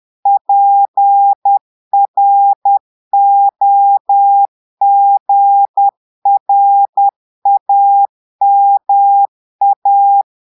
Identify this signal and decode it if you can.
Morse code, heard as a keyed tone, about 10 wpm, 800 Hz. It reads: PROGRAMA
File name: portugais texte en clair 2 vitesse 10 mots minutes